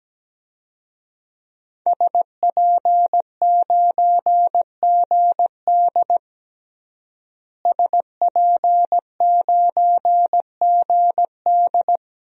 Morse code, heard as a keyed tone, about 17 wpm, 700 Hz. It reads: SP9GD SP9GD
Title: DXCC-100 Call Signs - 5 Characters - Repeated 3x - 17wpm